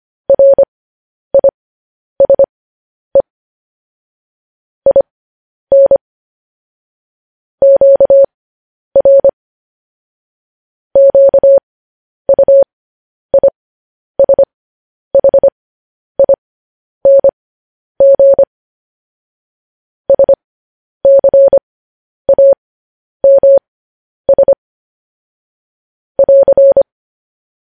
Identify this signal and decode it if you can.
Morse code, heard as a keyed tone, about 25 wpm, spaced as for 10 wpm, 550 Hz. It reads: RISE IN QR QUISHING SCAMS <AR>
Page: news headlines in morse code at 10 wpm